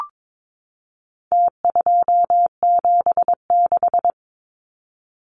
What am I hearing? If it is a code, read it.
Morse code, heard as a keyed tone, about 22 wpm, 700 Hz. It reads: T276
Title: Numbers - Contest Style - 4-Digit 22wpm